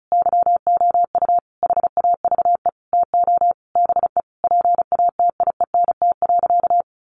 Morse code, heard as a keyed tone, about 34 wpm, 700 Hz. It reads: YOU HAVE TO BE PATIENT.
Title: Easy Phrases 35wpm